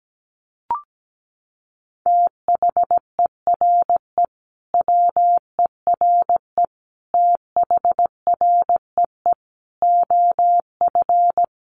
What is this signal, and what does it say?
Morse code, heard as a keyed tone, about 17 wpm, 700 Hz. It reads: THERE WERE THREE OF
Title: Sentences from Top 800 Words 17wpm